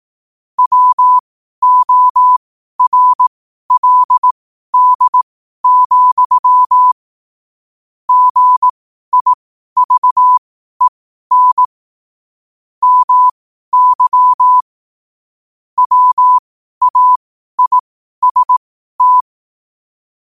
Morse code, heard as a keyed tone, about 18 wpm, spaced as for 12 wpm, 1000 Hz. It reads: WORLD, GIVEN MY WAIST
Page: New quotes every day in morse code at 12 Words per minute.